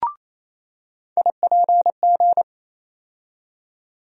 Morse code, heard as a keyed tone, about 28 wpm, 700 Hz. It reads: IPG
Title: USA Call Sign Postfixes - 3 Letters 28wpm